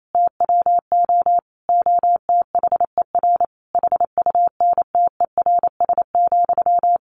Morse code, heard as a keyed tone, about 28 wpm, 700 Hz. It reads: TWO OTHER HUNTERS,
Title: A Sound Of Thunder 28wpm